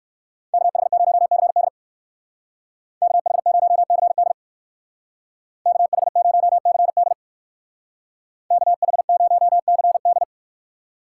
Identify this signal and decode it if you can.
Morse code, heard as a keyed tone, about 59 wpm, 700 Hz. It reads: KH0CD KH0CD KH0CD KH0CD